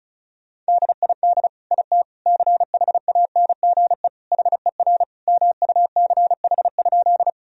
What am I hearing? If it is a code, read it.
Morse code, heard as a keyed tone, about 36 wpm, 700 Hz. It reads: DID IT CHANGE HER MUCH?
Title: Sentences from Top 1000 Words - Review Part 1 - 35wpm